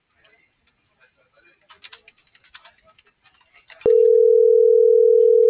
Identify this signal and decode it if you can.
Ringback tone (North America), 440 + 480 Hz.